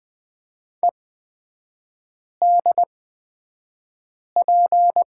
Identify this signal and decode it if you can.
Morse code, heard as a keyed tone, about 20 wpm, spaced as for 5 wpm, 700 Hz. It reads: EDP
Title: 243 Lesson 39e - Letter z - Three Characters 20wpm